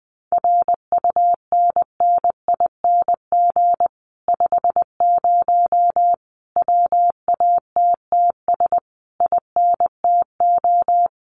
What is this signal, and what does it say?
Morse code, heard as a keyed tone, about 20 wpm, 700 Hz. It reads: RUNNING 50 WATTS INTO
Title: Common QSO Phrases Encore 20wpm